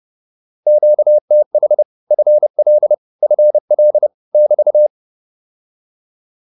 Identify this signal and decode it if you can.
Morse code, heard as a keyed tone, about 30 wpm, 600 Hz. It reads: QTH FL FL <BT>